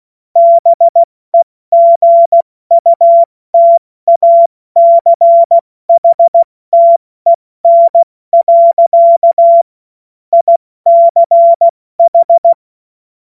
Morse code, heard as a keyed tone, about 16 wpm, 680 Hz. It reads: BEGUTACHTEN. ICH
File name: Morsekurs / Witze / Witz 10. Mai 2022 70 F80